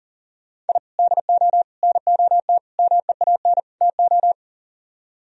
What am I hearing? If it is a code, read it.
Morse code, heard as a keyed tone, about 50 wpm, 700 Hz. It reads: I DO NOT MEAN TO